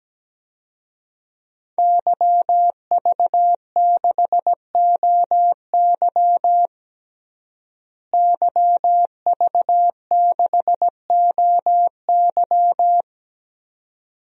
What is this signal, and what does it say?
Morse code, heard as a keyed tone, about 17 wpm, 700 Hz. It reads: YV6OY YV6OY
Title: DXCC-100 Call Signs - 5 Characters - Repeated 2x - 17wpm